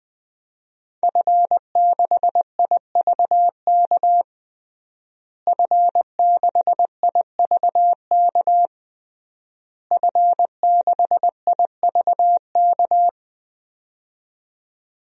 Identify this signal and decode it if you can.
Morse code, heard as a keyed tone, about 20 wpm, 700 Hz. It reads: F6IVK F6IVK F6IVK
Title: DXCC-100 Call Signs - 5 Characters - Repeated 3x - 20wpm